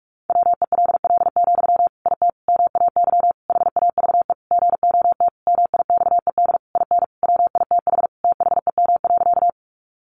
Morse code, heard as a keyed tone, about 44 wpm, 700 Hz. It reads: WELL, IT MAY HAVE GOT MIXED IN WITH THEM.
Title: Sentences from Top 1500 Words 45wpm